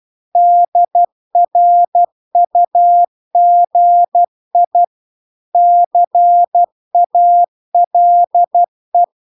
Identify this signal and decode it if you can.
Morse code, heard as a keyed tone, about 12 wpm, 700 Hz. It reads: DRUGI CALE